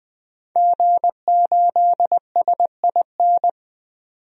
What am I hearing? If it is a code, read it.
Morse code, heard as a keyed tone, about 20 wpm, 700 Hz. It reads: G8SIN
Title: DXCC-100 Call Signs - 5 Characters - 20wpm